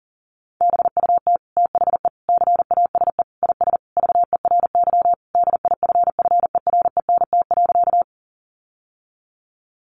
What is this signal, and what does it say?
Morse code, heard as a keyed tone, about 40 wpm, 700 Hz. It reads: BUT THE CASE IS VERY DIFFERENT.
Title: Sentences from Top 700 Words 40wpm